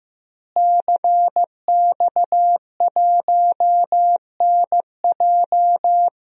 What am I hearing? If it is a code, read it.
Morse code, heard as a keyed tone, about 15 wpm, 700 Hz. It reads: CX1NJ
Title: DXCC-100 Call Signs - 5 Characters - Repeated 2x - 15wpm